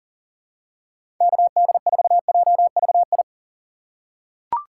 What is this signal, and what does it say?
Morse code, heard as a keyed tone, about 40 wpm, 700 Hz. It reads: KD4JVI
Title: USA Call Signs - 3x Character Spacing - 40wpm